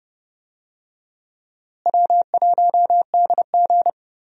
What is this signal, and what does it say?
Morse code, heard as a keyed tone, about 30 wpm, 700 Hz. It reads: W1DG
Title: USA Call Signs - 4 Characters - Repeated 3x - 30wpm